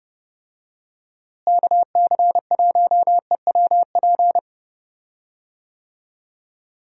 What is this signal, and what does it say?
Morse code, heard as a keyed tone, about 30 wpm, 700 Hz. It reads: KC1EWP
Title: USA Call Signs - 6 Characters - Repeated 2x - 30wpm